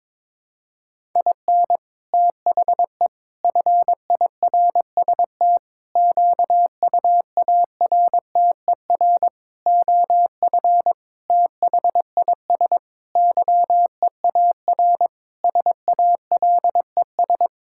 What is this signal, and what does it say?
Morse code, heard as a keyed tone, about 22 wpm, 700 Hz. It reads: IN THE FIRST QUARTER OF THIS YEAR SALES
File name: Sentences with 7-Word Phrases - 22wpm